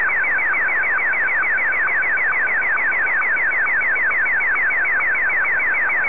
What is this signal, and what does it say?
SSTV (slow-scan television), Martin 1.
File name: sstv / M1-VK3AOB